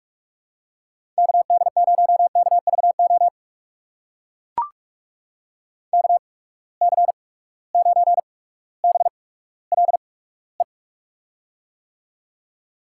Morse code, heard as a keyed tone, about 44 wpm, spaced as for 19 wpm, 700 Hz. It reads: KD0KVO KC9BLE
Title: USA Call Signs - 8x Character Spacing - 45wpm